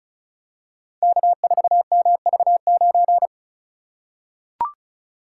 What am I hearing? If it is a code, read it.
Morse code, heard as a keyed tone, about 36 wpm, 700 Hz. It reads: K4MV9
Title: Code Groups - 5 Letter:Numbers - 35wpm